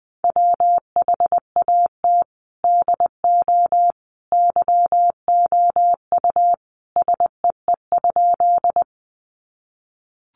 Morse code, heard as a keyed tone, about 20 wpm, 700 Hz. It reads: WHAT DO YOU SEE?
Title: Sentences from Top 100 Words 20wpm